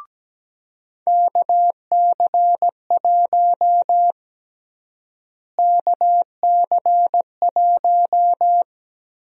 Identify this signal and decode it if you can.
Morse code, heard as a keyed tone, about 17 wpm, 700 Hz. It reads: KC1 KC1
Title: USA Call Sign Prefixes - Repeated 2x - 17wpm